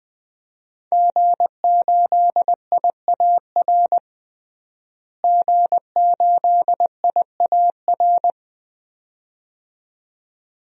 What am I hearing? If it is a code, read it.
Morse code, heard as a keyed tone, about 20 wpm, 700 Hz. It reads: G8IAR G8IAR
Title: DXCC-100 Call Signs - 5 Characters - Repeated 2x - 20wpm